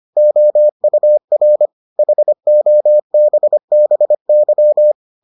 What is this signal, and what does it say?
Morse code, heard as a keyed tone, about 25 wpm, 600 Hz. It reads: OUR HOBBY